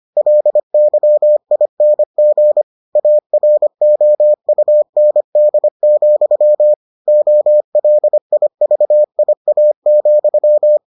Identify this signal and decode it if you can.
Morse code, heard as a keyed tone, about 25 wpm, 600 Hz. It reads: LYING AROUND, OLIVIA,